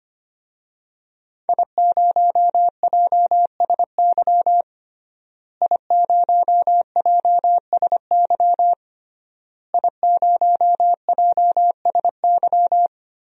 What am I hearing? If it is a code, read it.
Morse code, heard as a keyed tone, about 25 wpm, 700 Hz. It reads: I0JSY I0JSY I0JSY
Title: DXCC-100 Call Signs - 5 Characters - Repeated 3x - 25wpm